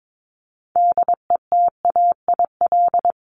Morse code, heard as a keyed tone, about 22 wpm, 700 Hz. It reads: DETAIL
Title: Top 1251-1500 Words 22wpm